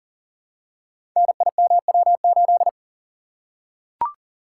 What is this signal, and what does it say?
Morse code, heard as a keyed tone, about 40 wpm, 700 Hz. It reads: NIMW8
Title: Code Groups - 5 Letter:Numbers - 40wpm